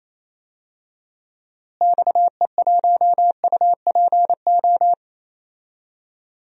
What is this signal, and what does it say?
Morse code, heard as a keyed tone, about 28 wpm, 700 Hz. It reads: XE1UPO
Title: DXCC-100 Call Signs - 6 Characters - Repeated 3x - 28wpm